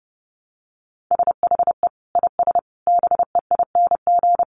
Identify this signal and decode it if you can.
Morse code, heard as a keyed tone, about 30 wpm, 700 Hz. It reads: SHE IS BEING
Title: Sentences from Top 600 Words 30wpm